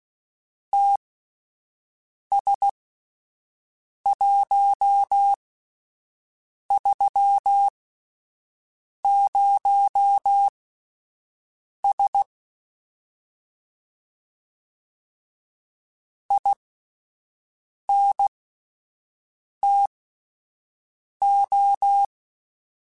Morse code, heard as a keyed tone, about 16 wpm, spaced as for 5 wpm, 800 Hz. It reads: TS130S INTO
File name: QSOs / QSOs-mp3s / WD0BGP 1@13